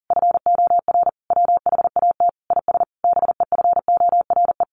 Morse code, heard as a keyed tone, about 40 wpm, 700 Hz. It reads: FOR WHAT IS BEFORE
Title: Sentences from Top 600 Words 40wpm